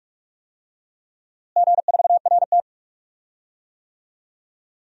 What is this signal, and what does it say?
Morse code, heard as a keyed tone, about 41 wpm, 700 Hz. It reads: G4RT
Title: DXCC-100 Call Signs - 4 Characters - Repeated 2x - 45wpm